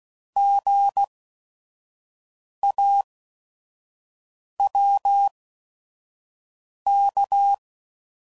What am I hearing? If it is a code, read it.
Morse code, heard as a keyed tone, about 16 wpm, spaced as for 5 wpm, 800 Hz. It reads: GAWK
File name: Prefixes / By First Letter / Gs